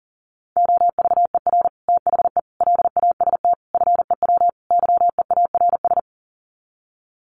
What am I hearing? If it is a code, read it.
Morse code, heard as a keyed tone, about 40 wpm, 700 Hz. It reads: OVER THE LAST FEW YEARS
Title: Sets of 5 Words Encore 40wpm